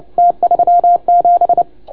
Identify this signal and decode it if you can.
Morse code, heard as a keyed tone, about 26 wpm, 690 Hz. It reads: T37